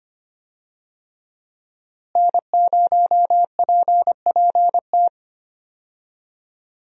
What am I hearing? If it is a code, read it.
Morse code, heard as a keyed tone, about 25 wpm, 700 Hz. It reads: N0PPT
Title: USA Call Signs - 5 Characters - Repeated 3x - 25wpm